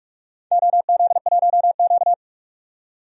Morse code, heard as a keyed tone, about 46 wpm, 700 Hz. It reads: OZ1Q
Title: DXCC-100 Call Signs - 4 Characters - 45wpm